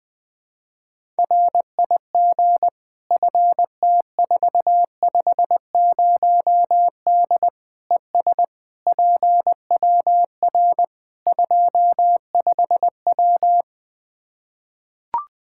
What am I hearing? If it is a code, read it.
Morse code, heard as a keyed tone, about 20 wpm, 700 Hz. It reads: RIG FT450D ES PWR 25W
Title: K1USN SST QSO Practice 20wpm